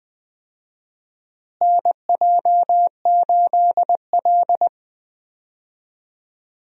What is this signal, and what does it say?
Morse code, heard as a keyed tone, about 20 wpm, 700 Hz. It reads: NJ8L
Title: USA Call Signs - 4 Characters - Repeated 3x - 20wpm